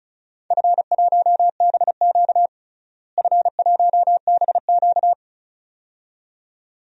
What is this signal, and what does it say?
Morse code, heard as a keyed tone, about 36 wpm, 700 Hz. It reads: F1BQ F1BQ
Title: DXCC-100 Call Signs - 4 Characters - Repeated 2x - 35wpm